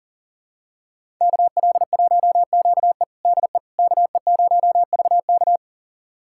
Morse code, heard as a keyed tone, about 40 wpm, 700 Hz. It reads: KL1QE DE KE0VK
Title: USA Callsign DE Callsign - 5 Characters - 40wpm